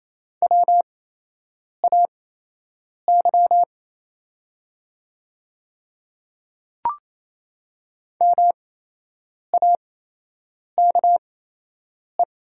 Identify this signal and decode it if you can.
Morse code, heard as a keyed tone, about 28 wpm, spaced as for 8 wpm, 700 Hz. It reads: WAY MAKE
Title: 8x Character Spacing - Top 100 Words - 28wpm